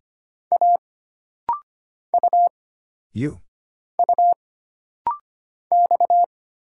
Morse code, heard as a keyed tone, about 25 wpm, 700 Hz. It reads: A U U X
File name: Single Character (Letter:Number:Punctuation:Prosign) - Rapid-Fire - 25wpm